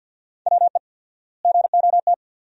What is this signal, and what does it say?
Morse code, heard as a keyed tone, about 50 wpm, 700 Hz. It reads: WE GOT
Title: 4x Word-Spacing - Sentences from Top 300 Words - 50wpm